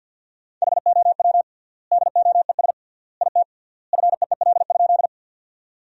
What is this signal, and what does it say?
Morse code, heard as a keyed tone, about 50 wpm, 700 Hz. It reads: HOW DOES IT FEEL?
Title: 3x Word-Spacing - Sentences from Top 400 Words - 50wpm